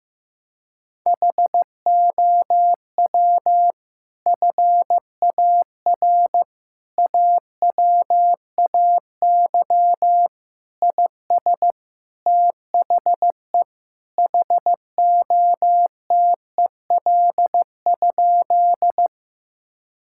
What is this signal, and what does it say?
Morse code, heard as a keyed tone, about 15 wpm, 700 Hz. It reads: HOW FAR AWAY IS THE HOTEL?
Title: Sentences from Top 1600 Words 15wpm